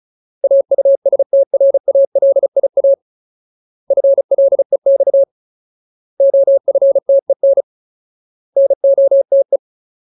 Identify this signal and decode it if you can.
Morse code, heard as a keyed tone, about 34 wpm, 550 Hz. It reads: AUSTRALIA FLEX OFTEN NOTE
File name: Ultimate Word List / 35WPM Ultimate